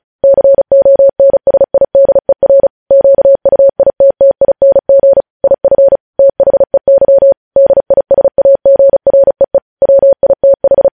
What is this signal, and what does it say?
Morse code, heard as a keyed tone, about 36 wpm, 550 Hz. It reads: CONSIDER QUITTING IF THEY DISAGREE WITH
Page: news headlines in morse code at 35 wpm